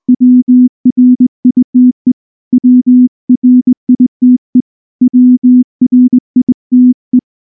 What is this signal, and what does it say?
Morse code, heard as a keyed tone, about 20 wpm, 250 Hz. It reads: WRITE WRITE WRITE
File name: morse / morse-training-20WPM-4